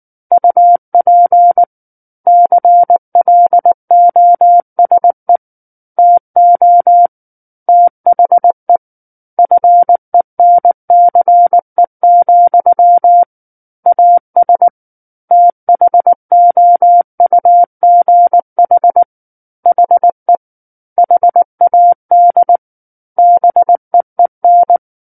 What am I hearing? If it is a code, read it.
Morse code, heard as a keyed tone, about 19 wpm, 700 Hz. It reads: UP CLOSE TO THE FENCE, AS THOUGH HE HAD BEEN
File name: Book One / 10-Chapter 10 - 19 WPM